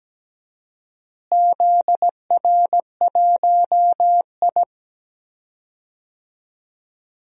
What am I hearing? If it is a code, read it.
Morse code, heard as a keyed tone, about 17 wpm, 700 Hz. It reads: ZR1I
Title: DXCC-100 Call Signs - 4 Characters - Repeated 3x - 17wpm